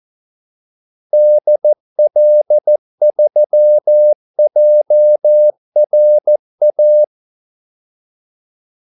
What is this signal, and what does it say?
Morse code, heard as a keyed tone, about 14 wpm, 600 Hz. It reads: DL3JRA
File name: Callsigns-14wpm